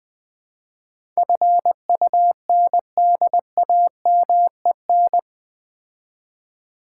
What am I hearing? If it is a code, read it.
Morse code, heard as a keyed tone, about 20 wpm, 700 Hz. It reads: FUNDAMEN
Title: Build A Word - 4 Letters At A Time - 11-Letter Words - 20wpm